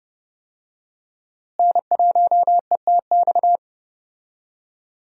Morse code, heard as a keyed tone, about 30 wpm, 700 Hz. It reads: N1ETX
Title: USA Call Signs - 5 Characters - Repeated 3x - 30wpm